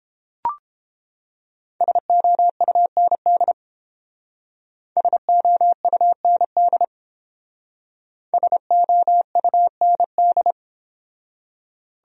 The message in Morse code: SOUND SOUND SOUND

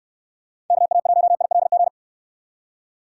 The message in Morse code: KA1ILG

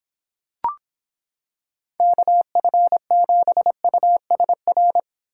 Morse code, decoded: KF7USR

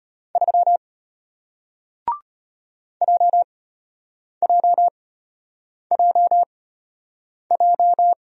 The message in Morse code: 3 JJJJ

